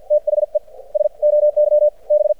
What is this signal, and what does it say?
Morse code, heard as a keyed tone, about 52 wpm, spaced as for 39 wpm, 600 Hz. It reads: THE SOYB
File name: archived / CRSnet